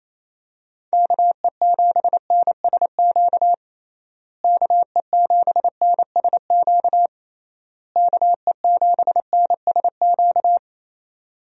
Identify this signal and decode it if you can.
Morse code, heard as a keyed tone, about 28 wpm, 700 Hz. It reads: KE7NSQ KE7NSQ KE7NSQ
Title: USA Call Signs - 6 Characters - Repeated 3x - 28wpm